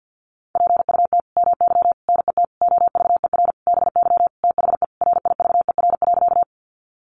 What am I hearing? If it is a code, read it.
Morse code, heard as a keyed tone, about 50 wpm, 700 Hz. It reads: PUT MY NET OVER BY THE RIVER.
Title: Sentences from Top 1400 Words 50wpm